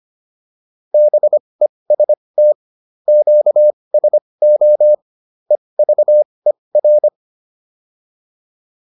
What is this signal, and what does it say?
Morse code, heard as a keyed tone, about 25 wpm, spaced as for 20 wpm, 600 Hz. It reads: BEST QSO EVER
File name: PR303 20